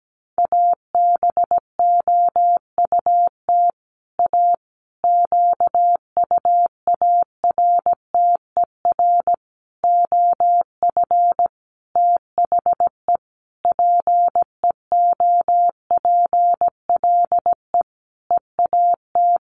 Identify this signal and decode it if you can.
Morse code, heard as a keyed tone, about 17 wpm, 700 Hz. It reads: ABOUT A QUARTER OF THE PEOPLE EAT
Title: Sentences from Top 1300 Words 17wpm